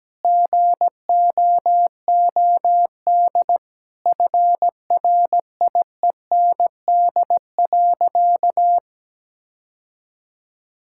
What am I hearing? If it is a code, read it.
Morse code, heard as a keyed tone, about 17 wpm, 700 Hz. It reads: GOOD FRIEND.
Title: Sentences from Top 1600 Words 17wpm